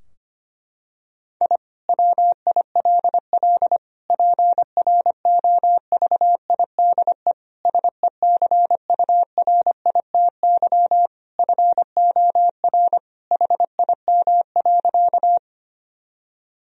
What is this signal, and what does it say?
Morse code, heard as a keyed tone, about 25 wpm, 700 Hz. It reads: I WILL PROVIDE SECURITY FOR HIM.
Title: Sentences from Top 1000 Words - Review Part 4 - 25wpm